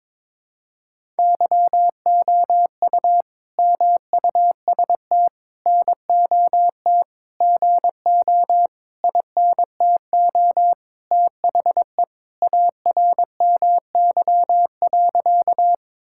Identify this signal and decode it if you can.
Morse code, heard as a keyed tone, about 22 wpm, 700 Hz. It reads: YOU MUST NOT GO INTO THE ARMY.
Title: Sentences from Top 1000 Words - Review Part 5 - 22wpm